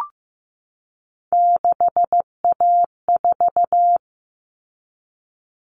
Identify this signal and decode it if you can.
Morse code, heard as a keyed tone, about 15 wpm, 700 Hz. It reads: 6A4
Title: Common Cut Numbers - 3-Digit 15wpm